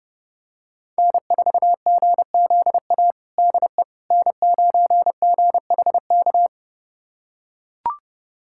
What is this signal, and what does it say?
Morse code, heard as a keyed tone, about 32 wpm, 700 Hz. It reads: N4GZA DE N9GHK
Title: USA Callsign DE Callsign - 5 Characters - 30wpm